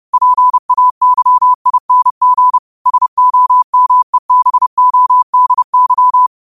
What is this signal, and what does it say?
Morse code, heard as a keyed tone, about 30 wpm, 1000 Hz. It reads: PAYING SOMEBODY
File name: qotdfiles / 30wpm